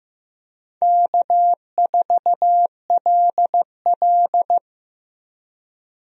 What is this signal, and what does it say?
Morse code, heard as a keyed tone, about 15 wpm, 700 Hz. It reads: K4LL
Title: USA Call Signs - 4 Characters - Repeated 3x - 15wpm